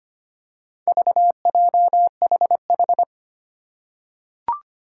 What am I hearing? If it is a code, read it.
Morse code, heard as a keyed tone, about 25 wpm, 700 Hz. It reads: VJHH
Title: Four Challenging Letter-Number Combinations 25wpm